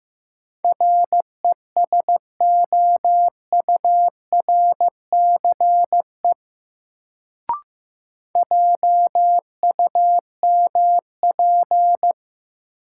Morse code, heard as a keyed tone, about 16 wpm, 700 Hz. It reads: RESOURCE JUMP
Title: Sentences from Top 1300 Words - New Word Drill - 15wpm